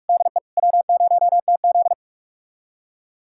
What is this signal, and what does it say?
Morse code, heard as a keyed tone, about 44 wpm, 690 Hz. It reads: DE W0TZ
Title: USA Callsign DE Callsign - 4 Characters - 45wpm